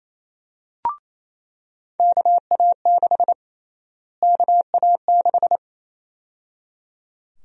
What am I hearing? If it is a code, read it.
Morse code, heard as a keyed tone, about 27 wpm, 700 Hz. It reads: KA6 KA6
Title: USA Call Sign Prefixes - Repeated 2x - 28wpm